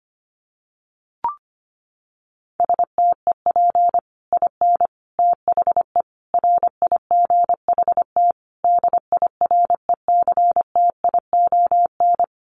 Morse code, heard as a keyed tone, about 25 wpm, 700 Hz. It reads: STEP IN THE RIGHT DIRECTION
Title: Sets of 5 Words Encore 25wpm